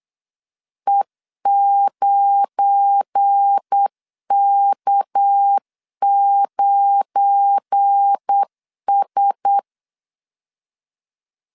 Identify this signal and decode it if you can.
Morse code, heard as a keyed tone, about 8 wpm, 800 Hz. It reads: E9K9S
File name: LEZIONI / 7 GF9Z / 7 GF9 8wpm 3545